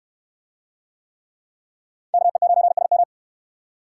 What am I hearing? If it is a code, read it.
Morse code, heard as a keyed tone, about 63 wpm, 700 Hz. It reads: KE9SM